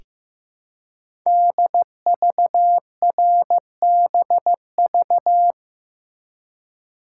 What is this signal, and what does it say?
Morse code, heard as a keyed tone, about 15 wpm, 700 Hz. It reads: DVRBV
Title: Code Groups - 5 Letters - 15wpm